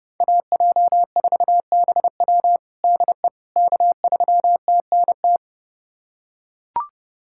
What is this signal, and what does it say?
Morse code, heard as a keyed tone, about 30 wpm, 700 Hz. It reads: AJ4BW DE K3TNT
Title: USA Callsign DE Callsign - 5 Characters - 30wpm